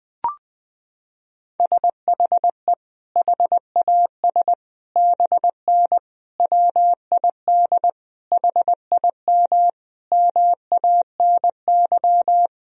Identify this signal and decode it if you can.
Morse code, heard as a keyed tone, about 20 wpm, 700 Hz. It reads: SHE HAS BN WID HIM MANY
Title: Sentences from Top 200 Words and Abbreviations 20wpm